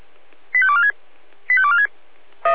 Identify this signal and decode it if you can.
Five-tone selective call (ZVEI), heard as quick runs of five tones, tones of about 70 ms. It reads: 85246 85246